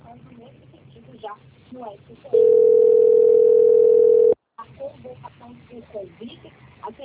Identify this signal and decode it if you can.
Ringback tone (North America), 440 + 480 Hz, with speech.
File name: out- - - - - .8443